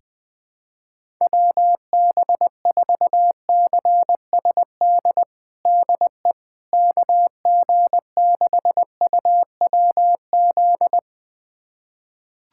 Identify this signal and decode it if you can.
Morse code, heard as a keyed tone, about 20 wpm, 700 Hz. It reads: WB4CSD DE KG6UWZ
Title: USA Callsign DE Callsign 20wpm